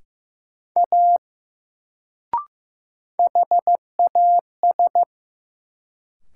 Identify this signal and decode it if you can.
Morse code, heard as a keyed tone, about 15 wpm, 700 Hz. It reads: A HAS
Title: Sentences from Top 100 Words - New Word Drill - 15wpm